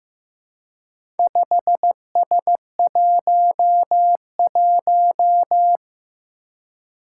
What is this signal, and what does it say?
Morse code, heard as a keyed tone, about 15 wpm, 700 Hz. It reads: 5S11
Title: Four Challenging Letter-Number Combinations 15wpm